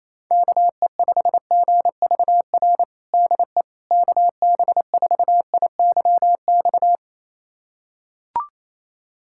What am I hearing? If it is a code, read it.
Morse code, heard as a keyed tone, about 28 wpm, 700 Hz. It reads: KE5GVR DE KB4IYX
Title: USA Callsign DE Callsign 28wpm